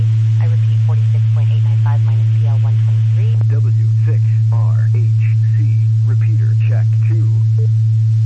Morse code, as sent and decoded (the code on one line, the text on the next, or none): none